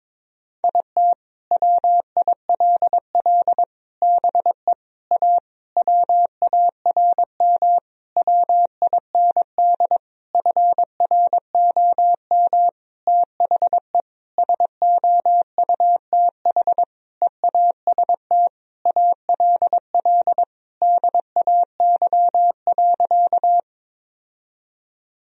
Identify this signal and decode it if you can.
Morse code, heard as a keyed tone, about 22 wpm, 700 Hz. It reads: IT WILL BE A WARM WIND FROM THE SOUTH EAST ALL DAY.
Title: Sentences from Top 1000 Words - Review Part 3 - 22wpm